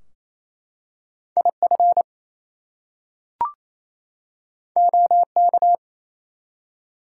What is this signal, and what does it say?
Morse code, heard as a keyed tone, about 28 wpm, 700 Hz. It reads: IF OK
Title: Two Letter Words - 28wpm